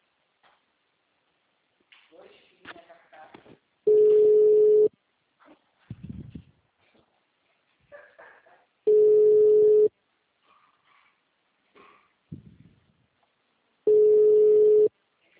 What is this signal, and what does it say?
Ringback tone (Europe, 425 Hz), with speech.